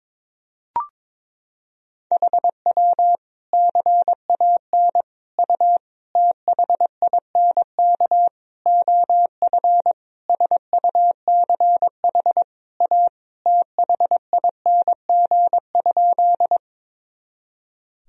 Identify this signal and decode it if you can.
Morse code, heard as a keyed tone, about 22 wpm, 700 Hz. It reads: HW CAN U THINK OF SUCH A THING?
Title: Sentences from Top 200 Words and Abbreviations 22wpm